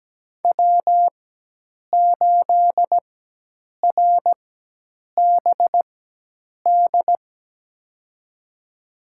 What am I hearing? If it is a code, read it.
Morse code, heard as a keyed tone, about 17 wpm, spaced as for 8 wpm, 700 Hz. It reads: W8RBD
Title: USA Call Signs - 5 Characters - 4x Character Spacing - 17wpm